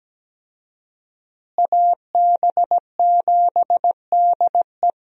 Morse code, heard as a keyed tone, about 17 wpm, 700 Hz. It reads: AB7DE
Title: USA Call Signs - 5 Characters - Repeated 3x - 17wpm